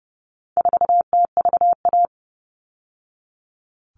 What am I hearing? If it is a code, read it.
Morse code, heard as a keyed tone, about 30 wpm, 700 Hz. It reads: 4TVA
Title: All Cut Numbers - 4-Digit 30wpm